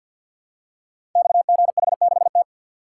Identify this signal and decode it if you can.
Morse code, heard as a keyed tone, about 51 wpm, 700 Hz. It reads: XGH6T
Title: Code Groups - 5 Letter:Numbers - 50wpm